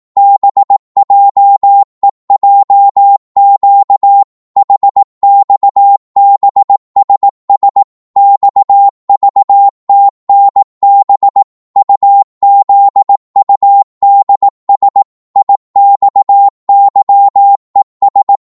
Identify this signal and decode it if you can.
Morse code, heard as a keyed tone, about 18 wpm, 800 Hz. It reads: BJEJQ HXBSS XVTNB UZUDS IXYES